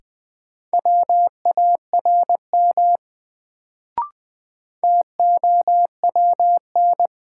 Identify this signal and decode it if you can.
Morse code, heard as a keyed tone, about 20 wpm, 700 Hz. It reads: WARM TOWN
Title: Sentences from Top 500 Words - New Word Drill - 20wpm